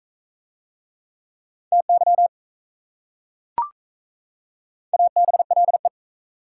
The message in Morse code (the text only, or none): TY ABLE